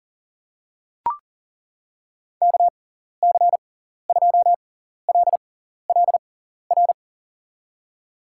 Morse code, heard as a keyed tone, about 40 wpm, spaced as for 14 wpm, 700 Hz. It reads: KC2LLR
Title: USA Call Signs - 6x Character Spacing - 40wpm